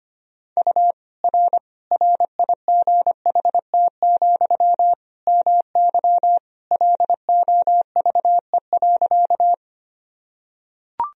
Morse code, heard as a keyed tone, about 25 wpm, 700 Hz. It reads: U R RIGHT, MY LOVE.
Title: Sentences from Top 400 Words and Abbreviations 25wpm